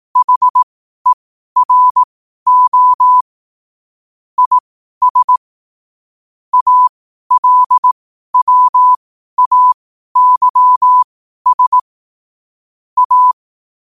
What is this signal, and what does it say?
Morse code, heard as a keyed tone, about 18 wpm, spaced as for 12 wpm, 1000 Hz. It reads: HERO IS ALWAYS A
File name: qotdfiles / 12wpm